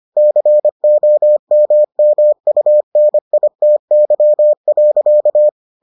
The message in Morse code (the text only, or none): COMMUNITY.